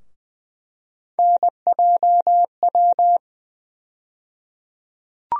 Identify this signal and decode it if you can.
Morse code, heard as a keyed tone, about 21 wpm, 700 Hz. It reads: NJW
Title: USA Call Sign Postfixes - 3 Letters 20wpm